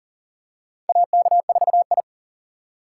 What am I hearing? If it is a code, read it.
Morse code, heard as a keyed tone, about 40 wpm, 700 Hz. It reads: AK4I
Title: USA Call Signs - 4 Characters - 4x Character Spacing - 40wpm